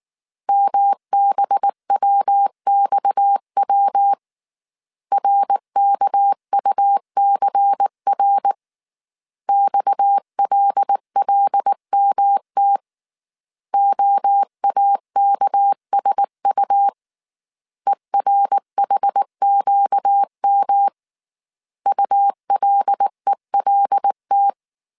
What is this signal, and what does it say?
Morse code, heard as a keyed tone, about 19 wpm, 800 Hz. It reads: MBWXW RKUCR XLLMT OAKSU ERHQM ULELT
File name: CW / DET / 1 6 1500C S A 18wpm 2255